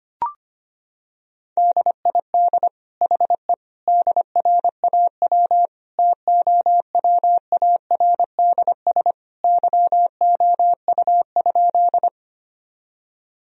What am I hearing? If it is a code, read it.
Morse code, heard as a keyed tone, about 25 wpm, 700 Hz. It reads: DID HE DRAW TOWARDS YOU?
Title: Sentences from Top 1000 Words - Review Part 2 - 25wpm